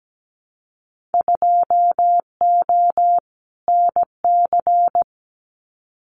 Morse code, heard as a keyed tone, about 17 wpm, 700 Hz. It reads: 2O NC
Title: Winter Field Day Exchanges 17wpm